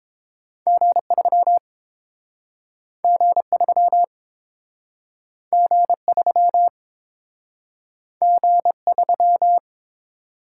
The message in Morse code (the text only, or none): G3 G3 G3 G3